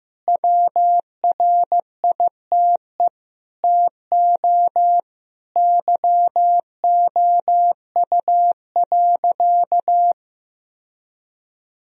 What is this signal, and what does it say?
Morse code, heard as a keyed tone, about 15 wpm, 700 Hz. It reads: WRITE TO YOU.
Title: Sentences from Top 1000 Words - Review Part 1 - 15wpm